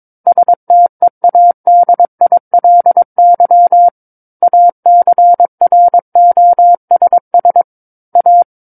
Morse code, heard as a keyed tone, about 22 wpm, 700 Hz. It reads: STEADILY ACROSS A
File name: Book One / 13-Chapter 13 - 22 WPM